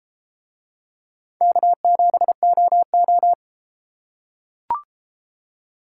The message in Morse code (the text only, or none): K7OO